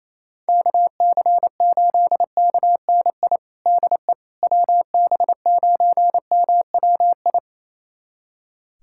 Morse code, heard as a keyed tone, about 28 wpm, 700 Hz. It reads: KC8KNI DE WB9MWI
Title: USA Callsign DE Callsign 28wpm